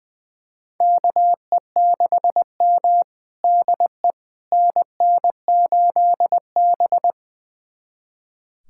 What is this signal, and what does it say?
Morse code, heard as a keyed tone, about 20 wpm, 700 Hz. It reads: KE6M DE NN8B
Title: USA Callsign DE Callsign - 4 Characters - 20wpm